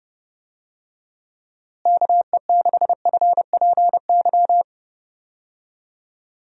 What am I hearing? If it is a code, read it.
Morse code, heard as a keyed tone, about 30 wpm, 700 Hz. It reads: KE6FPY
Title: USA Call Signs - 6 Characters - Repeated 3x - 30wpm